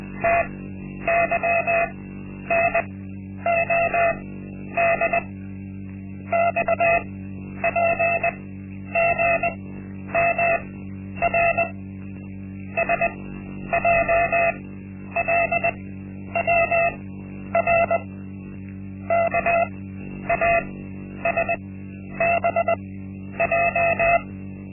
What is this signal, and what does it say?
Morse code, heard as a keyed tone, about 20 wpm, spaced as for 10 wpm, 680 Hz. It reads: TYNOD XPGMR SJLWR KASBJ
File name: Linksmakalnis-1